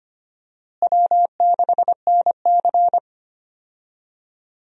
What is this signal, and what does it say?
Morse code, heard as a keyed tone, about 25 wpm, 700 Hz. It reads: W6NC
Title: USA Call Signs - 4 Characters - Repeated 3x - 25wpm